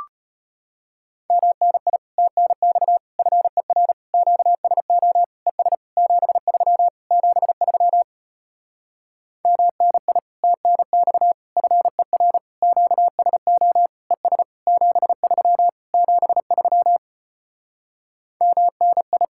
Morse code, heard as a keyed tone, about 33 wpm, 700 Hz. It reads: MNI TNX FER QSO ES 73 73 MNI TNX FER QSO ES 73 73 MNI